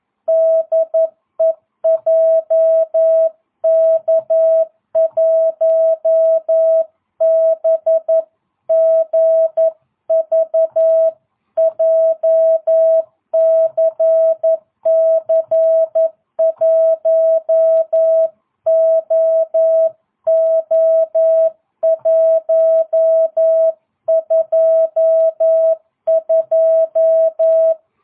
Morse code, heard as a keyed tone, about 11 wpm, 650 Hz. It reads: DE JK1BGV JCC1OO122